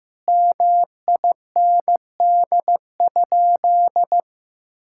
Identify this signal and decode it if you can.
Morse code, heard as a keyed tone, about 15 wpm, 700 Hz. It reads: MIND?
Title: Sentences from Top 1000 Words - Review Part 5 - 15wpm